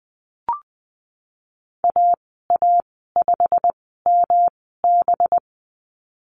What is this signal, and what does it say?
Morse code, heard as a keyed tone, about 20 wpm, spaced as for 15 wpm, 700 Hz. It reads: AA5MB
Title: 193 Lesson 31g - Letter v - Call Signs 20wpm